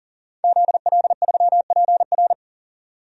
Morse code, heard as a keyed tone, about 40 wpm, 700 Hz. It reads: ZL3PR
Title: DXCC-100 Call Signs - 5 Characters - 40wpm